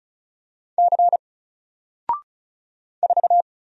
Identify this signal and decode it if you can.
Morse code, heard as a keyed tone, about 36 wpm, 700 Hz. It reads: C 4
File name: Single-Letter-Number-35wpm